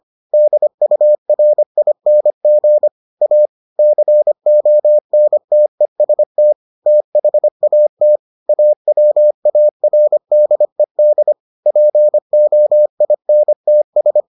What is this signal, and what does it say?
Morse code, heard as a keyed tone, about 25 wpm, 600 Hz. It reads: DURING A CONTEST THAT AWARDED POINTS